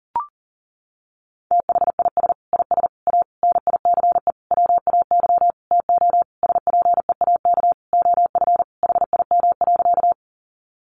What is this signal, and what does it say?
Morse code, heard as a keyed tone, about 40 wpm, 700 Hz. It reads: THIS IS A NICE WAY TO SPEAK OF HIM.
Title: Sentences from Top 1100 Words 40wpm